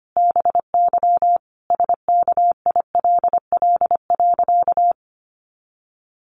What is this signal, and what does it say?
Morse code, heard as a keyed tone, about 25 wpm, 700 Hz. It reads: BY SKILL.
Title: Sentences from Top 1200 Words 25wpm